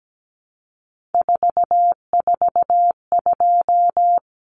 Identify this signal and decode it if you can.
Morse code, heard as a keyed tone, about 17 wpm, 700 Hz. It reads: 442
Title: Numbers - 3-Digit 17wpm